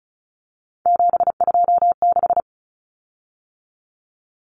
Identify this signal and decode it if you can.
Morse code, heard as a keyed tone, about 34 wpm, 700 Hz. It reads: 726
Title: Common Cut Numbers - 3-Digit 35wpm